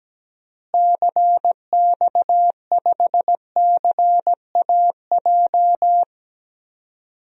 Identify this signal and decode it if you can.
Morse code, heard as a keyed tone, about 17 wpm, 700 Hz. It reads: CX5CAJ
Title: DXCC-100 Call Signs - 6 Characters - Repeated 2x - 17wpm